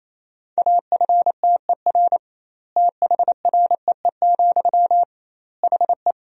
Morse code, heard as a keyed tone, about 28 wpm, 700 Hz. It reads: AFTER THREE, HE
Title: 2x Word-Spacing - Sentences from Top 200 Words - 28wpm